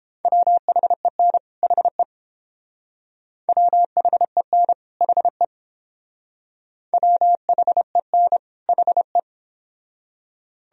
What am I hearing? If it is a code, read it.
Morse code, heard as a keyed tone, about 30 wpm, 700 Hz. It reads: WHEN HE WHEN HE WHEN HE